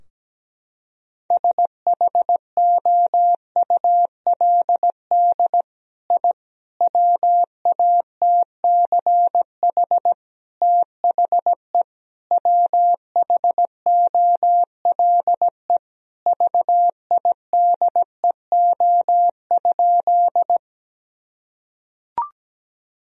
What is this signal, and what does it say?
Morse code, heard as a keyed tone, about 17 wpm, 700 Hz. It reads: SHOULD I WATCH THE WHOLE VIDEO?
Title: Sentences from Top 1000 Words - Review Part 4 - 17wpm